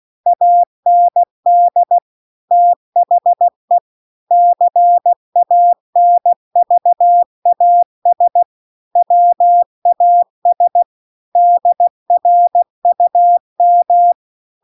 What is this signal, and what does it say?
Morse code, heard as a keyed tone, about 16 wpm, 700 Hz. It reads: AND THE CANVAS WAS DRUM